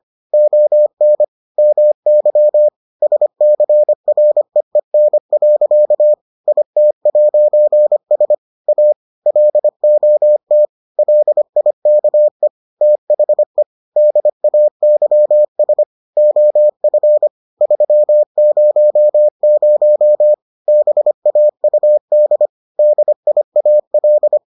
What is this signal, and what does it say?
Morse code, heard as a keyed tone, about 25 wpm, 600 Hz. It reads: ON MY SCREEN. IT'S A LOT LIKE THE DAYS OF 300 BAUD DIAL